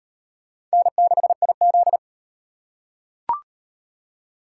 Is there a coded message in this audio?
Yes: Morse code from 0.7 to 2.0 s, heard as a keyed tone, about 38 wpm, 700 Hz. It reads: N6IZ